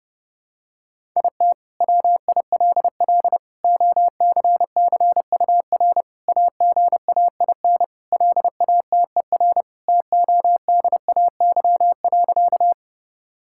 Morse code, heard as a keyed tone, about 30 wpm, 700 Hz. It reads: IT WILL OCCUR AGAIN LATER TODAY.
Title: Sentences from Top 1000 Words - Review Part 6 - 30wpm